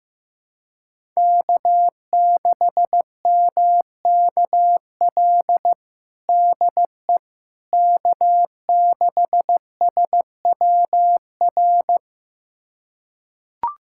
Morse code, heard as a keyed tone, about 15 wpm, 700 Hz. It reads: K6MKL DE K6SWR
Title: USA Callsign DE Callsign - 5 Characters - 15wpm